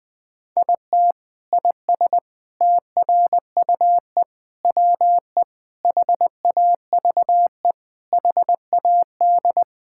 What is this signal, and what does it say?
Morse code, heard as a keyed tone, about 20 wpm, 700 Hz. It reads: IT IS TRUE WE HAVE HAD
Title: Sentences from Top 1000 Words - Review Part 5 - 20wpm